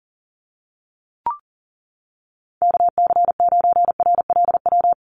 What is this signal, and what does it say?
Morse code, heard as a keyed tone, about 40 wpm, 700 Hz. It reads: KC9RLW
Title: USA Call Signs 40wpm